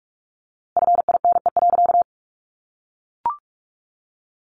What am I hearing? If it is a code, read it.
Morse code, heard as a keyed tone, about 44 wpm, 700 Hz. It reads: FINE.
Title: A Sound Of Thunder 45wpm